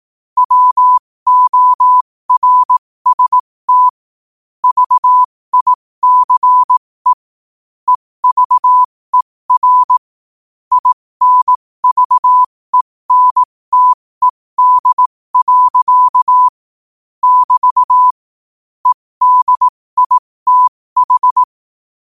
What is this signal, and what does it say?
Morse code, heard as a keyed tone, about 18 wpm, spaced as for 15 wpm, 1000 Hz. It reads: WORST VICE EVER INVENTED. <BT> EDITH
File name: qotdfiles / 15wpm